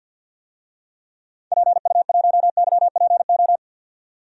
Morse code, heard as a keyed tone, about 48 wpm, 700 Hz. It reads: PU1YPO
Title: DXCC-100 Call Signs - 6 Characters - 50wpm